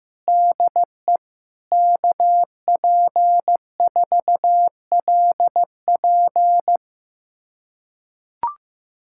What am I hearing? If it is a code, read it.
Morse code, heard as a keyed tone, about 15 wpm, 700 Hz. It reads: DE KP4LP
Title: USA Callsign DE Callsign 15wpm